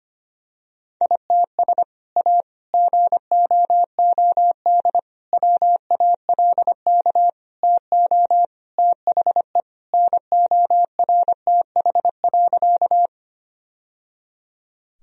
Morse code, heard as a keyed tone, about 25 wpm, 700 Hz. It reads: ITS A GOOD WALK TO THE NORTH.
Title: Sentences from Top 1000 Words - Review Part 1 - 25wpm